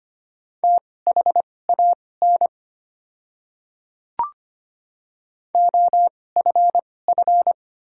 Morse code, heard as a keyed tone, about 25 wpm, spaced as for 18 wpm, 700 Hz. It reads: THAN OFF
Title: 2x Character Spacing - Top 200 Words - 25wpm